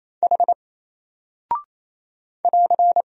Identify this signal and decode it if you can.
Morse code, heard as a keyed tone, about 27 wpm, 700 Hz. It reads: H <AR>
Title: Single Character (Letter:Number:Punctuation:Prosign) 28wpm